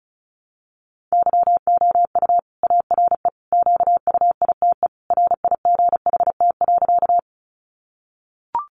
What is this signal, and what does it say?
Morse code, heard as a keyed tone, about 34 wpm, 700 Hz. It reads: YOU ARE QUITE RIGHT.
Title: Sentences from Top 700 Words 35wpm